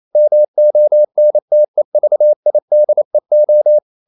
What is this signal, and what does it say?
Morse code, heard as a keyed tone, about 28 wpm, 600 Hz. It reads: MONTEVIDEO
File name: CW / Practice / NationalCapitals28